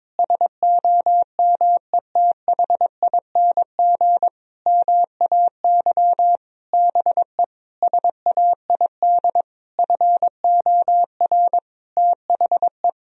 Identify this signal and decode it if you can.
Morse code, heard as a keyed tone, about 22 wpm, 700 Hz. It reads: SOMETHING MAY BE SAID FOR THE
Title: Sentences from Top 1000 Words - Review Part 3 - 22wpm